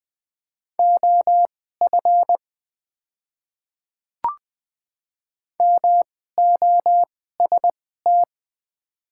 Morse code, heard as a keyed tone, about 20 wpm, spaced as for 14 wpm, 700 Hz. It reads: OF MOST
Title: 2x Character Spacing - Top 100 Words - 20wpm